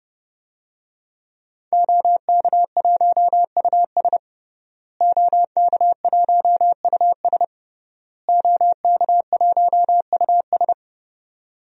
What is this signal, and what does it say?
Morse code, heard as a keyed tone, about 31 wpm, 700 Hz. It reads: OK1US OK1US OK1US
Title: DXCC-100 Call Signs - 5 Characters - Repeated 3x - 30wpm